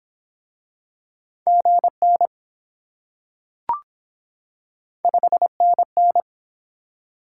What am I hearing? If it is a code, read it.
Morse code, heard as a keyed tone, about 23 wpm, 700 Hz. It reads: GN 5NN